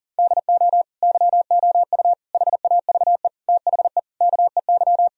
Morse code, heard as a keyed tone, about 40 wpm, 690 Hz. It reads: DO YOU HAVE THE KEY
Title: Sentences from Top 1000 Words - Review Part 4 - 40wpm